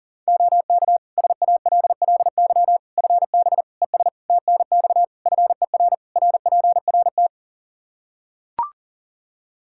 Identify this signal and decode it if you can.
Morse code, heard as a keyed tone, about 40 wpm, 700 Hz. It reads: OK SALLY FB ES TNX FER RPRT
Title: Practice QSO - Exchange 2 - Info - 40wpm